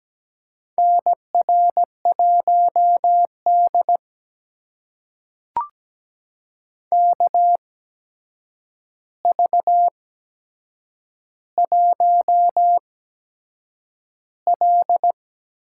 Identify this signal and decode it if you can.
Morse code, heard as a keyed tone, about 17 wpm, spaced as for 7 wpm, 700 Hz. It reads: NR1D KV1L
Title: USA Call Signs - 4 Characters - 8x Character Spacing - 17wpm